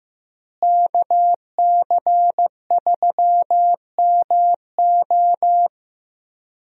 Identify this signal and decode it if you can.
Morse code, heard as a keyed tone, about 15 wpm, 700 Hz. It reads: KC3MO
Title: USA Call Signs - 5 Characters - Repeated 3x - 15wpm